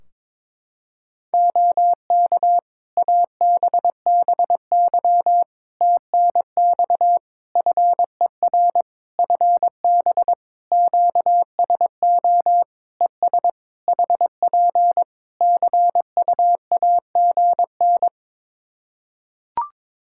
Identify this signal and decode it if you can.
Morse code, heard as a keyed tone, about 22 wpm, 700 Hz. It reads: OK ABBY TNX FER FB QSO ES HP CUAGN
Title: Practice QSO - Exchange 4 - Ending - 22wpm